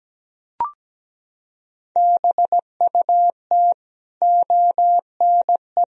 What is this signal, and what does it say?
Morse code, heard as a keyed tone, about 17 wpm, 700 Hz. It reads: BUT ONE
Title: Sentences from Top 1000 Words - Review Part 4 - 17wpm